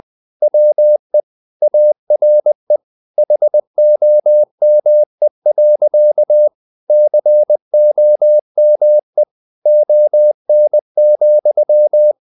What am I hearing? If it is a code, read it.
Morse code, heard as a keyed tone, about 20 wpm, 600 Hz. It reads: WE ARE HOME. COME ON,
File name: the old man and the sea part 2 20wpm